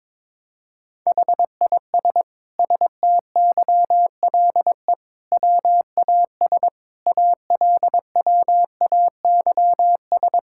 Morse code, heard as a keyed tone, about 22 wpm, 700 Hz. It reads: HIS STYLE WAS ALWAYS
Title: Sentences from Top 1000 Words - Review Part 5 - 22wpm